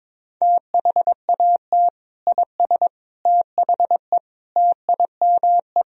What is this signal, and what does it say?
Morse code, heard as a keyed tone, about 21 wpm, 700 Hz. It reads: THAT IS THE TIME
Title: Sentences from Top 1000 Words - Review Part 1 - 22wpm